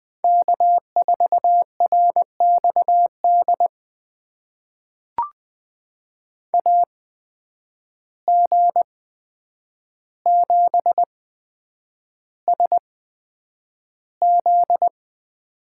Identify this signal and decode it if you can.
Morse code, heard as a keyed tone, about 20 wpm, spaced as for 9 wpm, 700 Hz. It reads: K4RXD AG7SZ
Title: USA Call Signs - 5 Characters - 8x Character Spacing - 20wpm